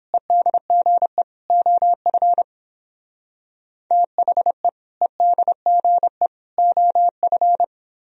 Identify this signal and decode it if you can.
Morse code, heard as a keyed tone, about 31 wpm, 700 Hz. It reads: EDGE OF THE EDGE OF